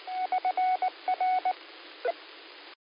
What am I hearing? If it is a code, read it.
Morse code, heard as a keyed tone, about 19 wpm, 740 Hz, with radio noise. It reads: <SN>R
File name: qsorec SimplexLogic 19-00-28